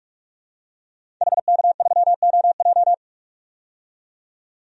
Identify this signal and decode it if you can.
Morse code, heard as a keyed tone, about 46 wpm, 700 Hz. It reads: HK3OJ
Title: DXCC-100 Call Signs - 5 Characters - 45wpm